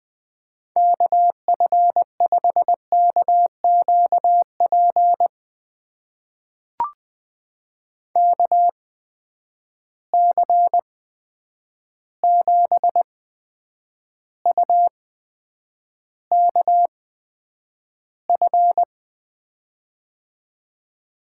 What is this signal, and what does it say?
Morse code, heard as a keyed tone, about 20 wpm, spaced as for 9 wpm, 700 Hz. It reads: KF5KQP KC7UKF
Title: USA Call Signs - 6 Characters - 8x Character Spacing - 20wpm